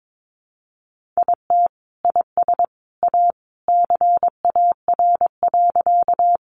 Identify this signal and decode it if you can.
Morse code, heard as a keyed tone, about 22 wpm, 700 Hz. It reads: IT IS A CAR.
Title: Easy Phrases 22wpm